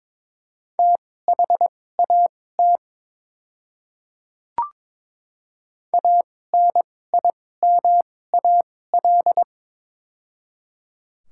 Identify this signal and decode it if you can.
Morse code, heard as a keyed tone, about 22 wpm, spaced as for 16 wpm, 700 Hz. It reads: THAT ANIMAL
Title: 2x Character Spacing - Top 500 Words - 22wpm